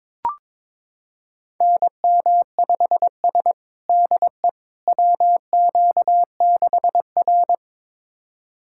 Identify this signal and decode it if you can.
Morse code, heard as a keyed tone, about 22 wpm, 700 Hz. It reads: NM5S DE WQ6R
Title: USA Callsign DE Callsign - 4 Characters - 22wpm